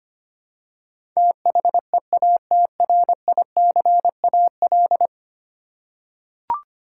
Morse code, heard as a keyed tone, about 25 wpm, 700 Hz. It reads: THEATRICAL
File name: 10 Letter Words - First 3 Letters Given - 25wpm